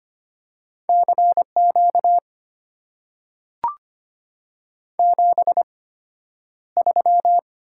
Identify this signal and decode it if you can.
Morse code, heard as a keyed tone, about 25 wpm, spaced as for 11 wpm, 700 Hz. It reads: CQ 73
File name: Top 90 QSO Elements - 8x Character Spacing - 25wpm